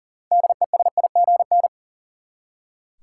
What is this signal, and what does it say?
Morse code, heard as a keyed tone, about 40 wpm, 700 Hz. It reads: DESIGN
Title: Six Letter Words - 40wpm